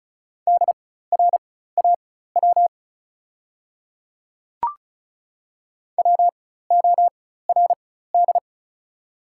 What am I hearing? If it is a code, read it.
Morse code, heard as a keyed tone, about 36 wpm, spaced as for 16 wpm, 690 Hz. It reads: DRAW WORD
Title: 4x Character Spacing - Top 500 Words - 35wpm